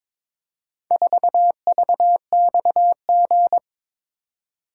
Morse code, heard as a keyed tone, about 22 wpm, 690 Hz. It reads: 4VXG
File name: practice / 22wpm / Four Letter-Number Combinations - 22wpm